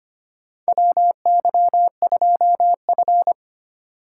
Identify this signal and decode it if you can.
Morse code, heard as a keyed tone, about 25 wpm, 700 Hz. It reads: WY2F
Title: USA Call Signs - 4 Characters - 2x Character Spacing - 25wpm